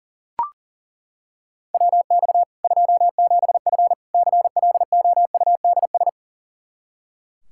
Morse code, heard as a keyed tone, about 40 wpm, 700 Hz. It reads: WX 27F CLOUDS
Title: K1USN SST QSO Practice 40wpm